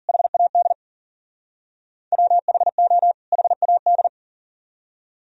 Morse code, heard as a keyed tone, about 50 wpm, 700 Hz. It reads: HAD WHO HAD
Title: Speed-Racing - Sets of 2 Words 40wpm